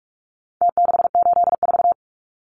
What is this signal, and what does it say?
Morse code, heard as a keyed tone, about 44 wpm, 700 Hz. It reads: T684
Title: Numbers - Contest Style - 4-Digit 45wpm